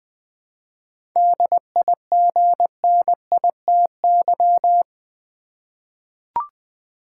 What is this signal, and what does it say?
Morse code, heard as a keyed tone, about 20 wpm, 690 Hz. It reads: DIGNITY
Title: 7 Letter Words - First 3 Letters Given - 20wpm